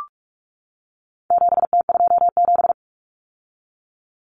Morse code, heard as a keyed tone, about 44 wpm, 700 Hz. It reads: 7T27
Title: Common Cut Numbers - 4-Digit 45wpm